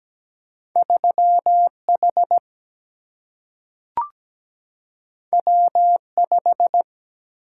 Morse code, heard as a keyed tone, about 17 wpm, 700 Hz. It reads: 3 H W 5
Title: Single-Letter Single-Number 17wpm